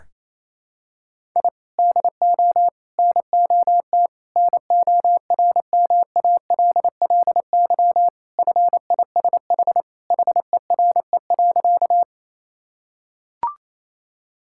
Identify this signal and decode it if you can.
Morse code, heard as a keyed tone, about 28 wpm, 700 Hz. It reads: I DO NOT NORMALLY FISH HERE.
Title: Sentences from Top 1000 Words - Review Part 6 - 28wpm